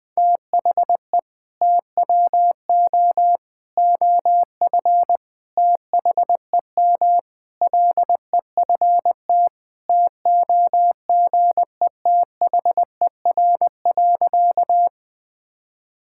Morse code, heard as a keyed tone, about 20 wpm, 700 Hz. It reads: THE TWO OF THEM LEFT TOGETHER.
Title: Sentences with 4-Word Phrases - Spoken Beginning - 20wpm